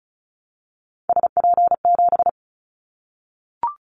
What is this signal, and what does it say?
Morse code, heard as a keyed tone, about 34 wpm, 700 Hz. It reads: SP7
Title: Callsign Prefixes for Top 25 DXCC - Repeated 3x - 35wpm